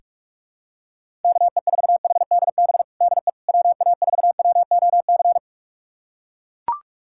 Morse code, heard as a keyed tone, about 42 wpm, 700 Hz. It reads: KE4HDB DE WA4WOC
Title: USA Callsign DE Callsign - 6 Characters - 45wpm